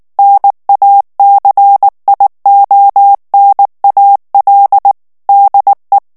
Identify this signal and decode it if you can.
Morse code, heard as a keyed tone, about 19 wpm, 800 Hz. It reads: NACIONAL DE
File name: morse mp3 / ESPAGNOL TEXTE EN CLAIR 2 VITESSE 18 MOTS MINUTES